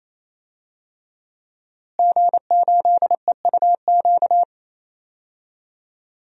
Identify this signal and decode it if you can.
Morse code, heard as a keyed tone, about 27 wpm, 700 Hz. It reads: G8EUQ
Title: DXCC-100 Call Signs - 5 Characters - Repeated 3x - 28wpm